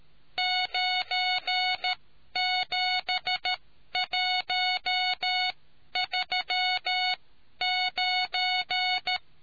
Morse code, heard as a keyed tone, about 13 wpm, spaced as for 11 wpm, 740 Hz. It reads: 97139